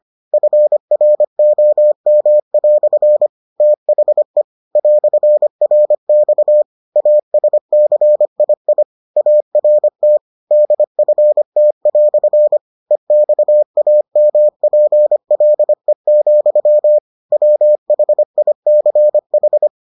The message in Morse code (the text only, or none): FROM" THE "RX ASCII ART DFT" EXAMPLE, WHICH